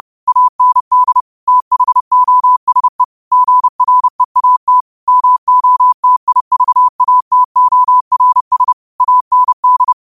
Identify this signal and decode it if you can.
Morse code, heard as a keyed tone, about 30 wpm, 1000 Hz. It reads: AND THOSE GREAT MOTIVATORS AND
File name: qotdfiles / 30wpm